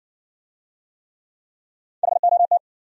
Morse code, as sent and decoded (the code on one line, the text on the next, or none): .... --- -
HOT